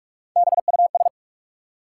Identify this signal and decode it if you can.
Morse code, heard as a keyed tone, about 46 wpm, 700 Hz. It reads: BUS
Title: Three Letter Words - 45wpm